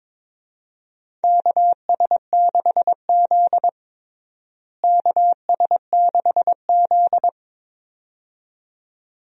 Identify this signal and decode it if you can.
Morse code, heard as a keyed tone, about 22 wpm, 700 Hz. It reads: KS6Z KS6Z
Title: USA Call Signs - 4 Characters - Repeated 2x - 22wpm